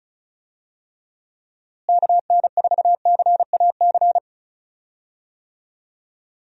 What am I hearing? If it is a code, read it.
Morse code, heard as a keyed tone, about 34 wpm, 700 Hz. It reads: KN4CAC
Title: USA Call Signs - 6 Characters - Repeated 2x - 35wpm